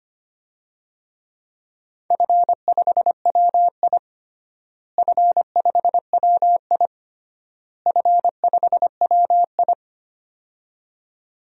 Morse code, heard as a keyed tone, about 25 wpm, 700 Hz. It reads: F5WI F5WI F5WI
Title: DXCC-100 Call Signs - 4 Characters - Repeated 3x - 25wpm